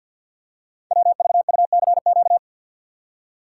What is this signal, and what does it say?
Morse code, heard as a keyed tone, about 48 wpm, 690 Hz. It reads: WVUCQ
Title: Code Groups - 5 Letters - 50wpm